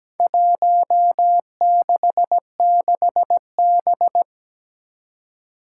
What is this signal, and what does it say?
Morse code, heard as a keyed tone, about 17 wpm, 700 Hz. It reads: 166B
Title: Four Challenging Letter-Number Combinations 17wpm